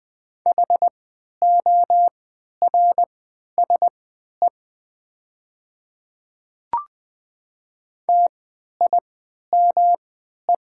Morse code, heard as a keyed tone, about 20 wpm, spaced as for 11 wpm, 690 Hz. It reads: HORSE TIME
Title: 3x Character Spacing - Top 300 Words - 20wpm